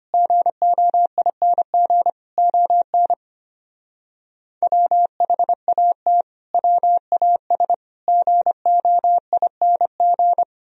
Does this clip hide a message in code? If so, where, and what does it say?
Yes: Morse code from 0.1 to 10.4 s, heard as a keyed tone, about 32 wpm, 700 Hz. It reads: GOING ON WHAT WAS GOING